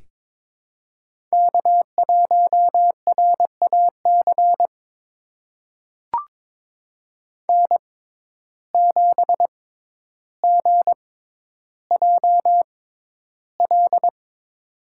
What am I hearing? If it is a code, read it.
Morse code, heard as a keyed tone, about 22 wpm, spaced as for 11 wpm, 700 Hz. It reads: K1RAC N7GJL
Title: USA Call Signs - 5 Characters - 6x Character Spacing - 22wpm